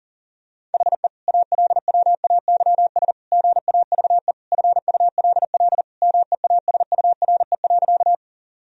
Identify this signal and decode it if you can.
Morse code, heard as a keyed tone, about 37 wpm, 700 Hz. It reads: HE ALWAYS GAVE FULL MEASURE.
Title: Sentences from Top 1000 Words - Review Part 3 - 40wpm